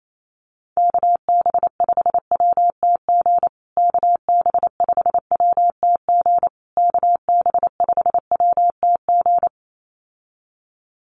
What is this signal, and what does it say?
Morse code, heard as a keyed tone, about 28 wpm, 700 Hz. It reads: KB5WTG KB5WTG KB5WTG
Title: USA Call Signs Repeated 3x 28wpm